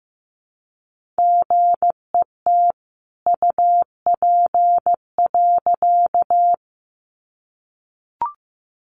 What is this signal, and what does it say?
Morse code, heard as a keyed tone, about 15 wpm, 700 Hz. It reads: GET UP.
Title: A Sound Of Thunder 15wpm